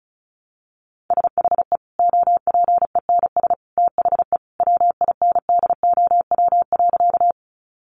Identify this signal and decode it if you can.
Morse code, heard as a keyed tone, about 34 wpm, 700 Hz. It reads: SHE OPENS THE WINDOW.
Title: Additional Easy Phrases 35wpm